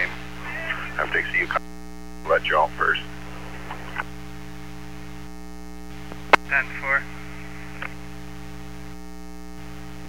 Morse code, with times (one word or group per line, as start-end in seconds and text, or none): none